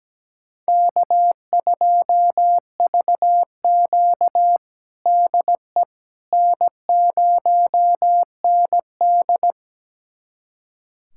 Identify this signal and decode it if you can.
Morse code, heard as a keyed tone, about 17 wpm, 700 Hz. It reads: K2VQ DE N0ND
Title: USA Callsign DE Callsign - 4 Characters - 17wpm